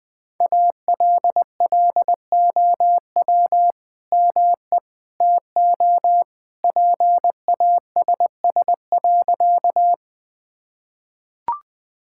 Morse code, heard as a keyed tone, about 20 wpm, 700 Hz. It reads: ALLOW ME TO PASS.
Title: Sentences from Top 1000 Words - Review Part 5 - 20wpm